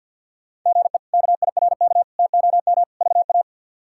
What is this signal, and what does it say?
Morse code, heard as a keyed tone, about 46 wpm, 700 Hz. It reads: GE KIRK TOM VA
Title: K1USN SST QSO Practice - 50wpm